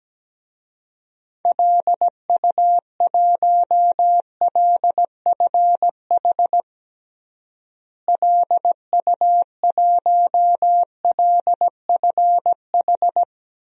Morse code, heard as a keyed tone, about 17 wpm, 700 Hz. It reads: LU1LFH LU1LFH
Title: DXCC-100 Call Signs - 6 Characters - Repeated 3x - 17wpm